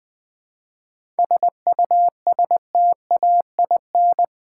Morse code, heard as a keyed tone, about 20 wpm, 700 Hz. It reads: SUSTAIN
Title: 7 Letter Words - First 3 Letters Given - 20wpm